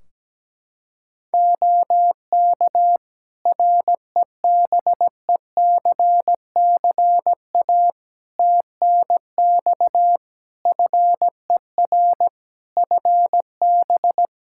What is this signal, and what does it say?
Morse code, heard as a keyed tone, about 17 wpm, 700 Hz. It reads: OK REBECCA TNX FER FB
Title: Practice QSO - Exchange 4 - Ending - 17wpm